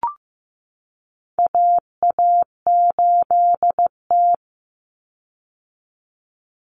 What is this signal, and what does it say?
Morse code, heard as a keyed tone, about 15 wpm, 700 Hz. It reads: AA8T
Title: Common Cut Numbers - 4-Digit 15wpm